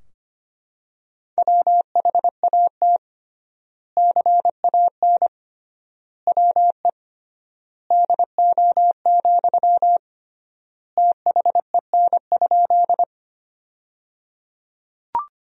Morse code, heard as a keyed tone, about 25 wpm, 700 Hz. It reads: WHAT CAN WE DO, THEN?
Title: 3x Word-Spacing - Sentences from Top 100 Words - 25wpm